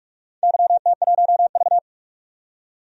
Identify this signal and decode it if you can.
Morse code, heard as a keyed tone, about 46 wpm, 700 Hz. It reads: YT1V
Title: DXCC-100 Call Signs - 4 Characters - 45wpm